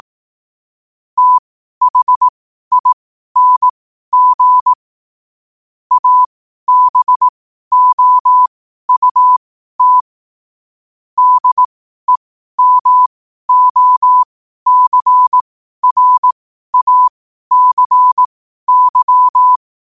Morse code, heard as a keyed tone, about 18 wpm, spaced as for 12 wpm, 1000 Hz. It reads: THING ABOUT DEMOCRACY
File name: qotdfiles / 12wpm